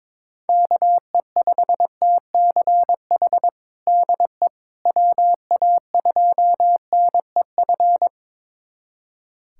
Morse code, heard as a keyed tone, about 22 wpm, 700 Hz. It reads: KE5TCH DE WA2NEF
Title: USA Callsign DE Callsign 22wpm